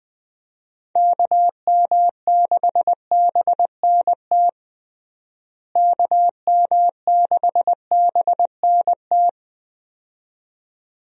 Morse code, heard as a keyed tone, about 20 wpm, 700 Hz. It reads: KM6BNT KM6BNT
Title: USA Call Signs - 6 Characters - Repeated 2x - 20wpm